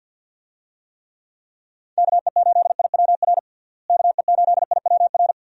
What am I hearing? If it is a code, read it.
Morse code, heard as a keyed tone, about 48 wpm, 700 Hz. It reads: KE8IWR KE8IWR
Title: USA Call Signs - 6 Characters - Repeated 2x - 50wpm